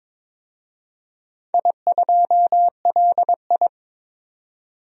Morse code, heard as a keyed tone, about 22 wpm, 700 Hz. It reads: I2LI
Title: DXCC-100 Call Signs - 4 Characters - 22wpm